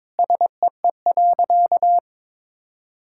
Morse code, heard as a keyed tone, about 22 wpm, 700 Hz. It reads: SEE.
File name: practice / 22wpm / Sentences with 3-Word Phrases - 22wpm